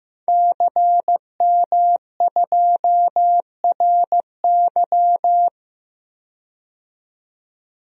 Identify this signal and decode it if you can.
Morse code, heard as a keyed tone, about 15 wpm, 700 Hz. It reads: CM2RY
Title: DXCC-100 Call Signs - 5 Characters - 15wpm